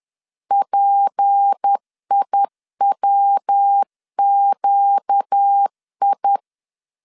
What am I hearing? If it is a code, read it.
Morse code, heard as a keyed tone, about 10 wpm, 800 Hz. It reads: PIWQI